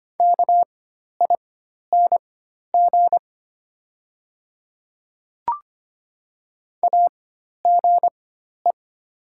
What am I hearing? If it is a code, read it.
Morse code, heard as a keyed tone, about 25 wpm, spaced as for 12 wpm, 690 Hz. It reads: KING AGE
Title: 4x Character Spacing - Top 500 Words - 25wpm